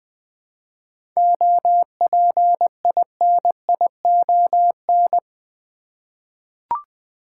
Morse code, heard as a keyed tone, about 20 wpm, 700 Hz. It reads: OPINION
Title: 7 Letter Words - First 3 Letters Given - 20wpm